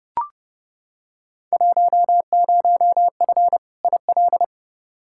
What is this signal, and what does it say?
Morse code, heard as a keyed tone, about 32 wpm, 700 Hz. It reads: 10F IL
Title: Field Day Exchanges Encore - 30wpm